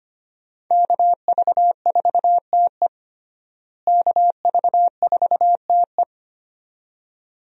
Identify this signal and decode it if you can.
Morse code, heard as a keyed tone, about 25 wpm, 700 Hz. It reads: KV4TE KV4TE
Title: USA Call Signs - 5 Characters - Repeated 2x - 25wpm